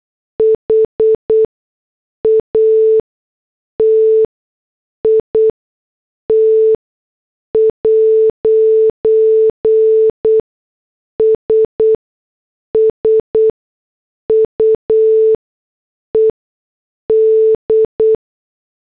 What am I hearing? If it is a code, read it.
Morse code, heard as a keyed tone, about 8 wpm, spaced as for 6 wpm, 440 Hz. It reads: HAT IT'S SUED